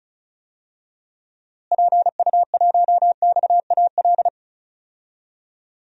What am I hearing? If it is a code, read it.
Morse code, heard as a keyed tone, about 36 wpm, 700 Hz. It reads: PU1XAL
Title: DXCC-100 Call Signs - 6 Characters - Repeated 3x - 35wpm